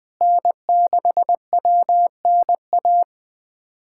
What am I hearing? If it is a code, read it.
Morse code, heard as a keyed tone, about 20 wpm, 700 Hz. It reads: N6WNA
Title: USA Callsign DE Callsign - 5 Characters - 20wpm